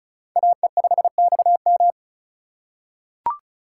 Morse code, heard as a keyed tone, about 34 wpm, 700 Hz. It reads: AE5XM
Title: USA Call Signs - 5 Characters - 2x Character Spacing - 35wpm